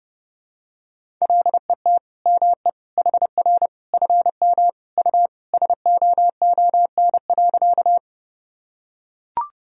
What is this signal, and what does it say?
Morse code, heard as a keyed tone, about 30 wpm, 700 Hz. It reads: LET ME HR FM U SOON.
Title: Sentences from Top 400 Words and Abbreviations 30wpm